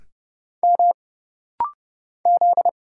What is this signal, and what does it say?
Morse code, heard as a keyed tone, about 30 wpm, 700 Hz. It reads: M Z
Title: Single Letter-Number Alt - Rapid-Fire 30wpm